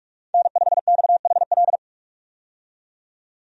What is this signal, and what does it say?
Morse code, heard as a keyed tone, about 47 wpm, 700 Hz. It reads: N5XHL
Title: USA Call Signs - 5 Characters - 45wpm